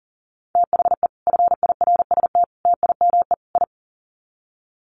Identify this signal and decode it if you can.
Morse code, heard as a keyed tone, about 40 wpm, 700 Hz. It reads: THE FIRST TIME I
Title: Sets of 4 Words 40wpm